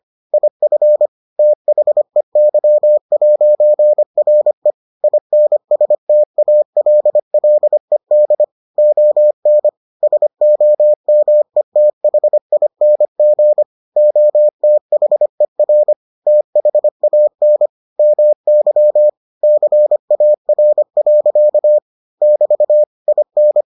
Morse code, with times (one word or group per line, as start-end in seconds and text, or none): 0.3-1.1 IF
1.4-4.7 THEY'RE
5.0-8.4 INSTALLED
8.8-9.7 ON
10.0-13.6 SOMETHING
14.0-15.9 OTHER
16.3-17.7 THAN
18.0-19.1 MY
19.4-21.8 CAR.
22.2-22.8 <BT>
23.1-23.6 IN